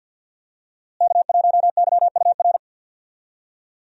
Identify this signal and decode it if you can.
Morse code, heard as a keyed tone, about 52 wpm, 700 Hz. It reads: K1YUR
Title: USA Call Signs - 5 Characters - Repeated 3x - 50wpm